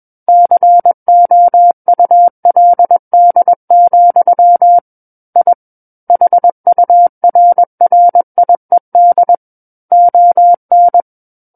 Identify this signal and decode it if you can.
Morse code, heard as a keyed tone, about 21 wpm, 700 Hz. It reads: COULD, I HURRIED ON
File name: Book One / 12-Chapter 12 - 21 WPM